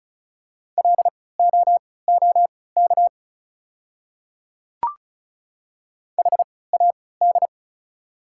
Morse code, heard as a keyed tone, about 36 wpm, spaced as for 20 wpm, 690 Hz. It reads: LOOK HAD
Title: 3x Character Spacing - Top 100 Words - 35wpm